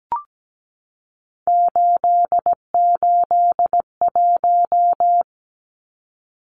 Morse code, heard as a keyed tone, about 17 wpm, 700 Hz. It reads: 881
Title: Numbers - 3-Digit 17wpm